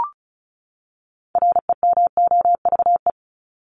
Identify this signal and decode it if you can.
Morse code, heard as a keyed tone, about 34 wpm, 700 Hz. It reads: REMOVE
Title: Top 751-1000 Words 35wpm